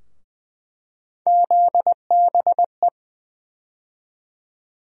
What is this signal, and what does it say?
Morse code, heard as a keyed tone, about 20 wpm, 700 Hz. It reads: ZBE
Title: USA Call Sign Postfixes - 3 Letters 20wpm